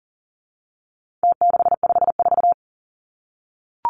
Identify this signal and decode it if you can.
Morse code, heard as a keyed tone, about 40 wpm, 700 Hz. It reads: T654
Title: Numbers - Contest Style - 4-Digit 40wpm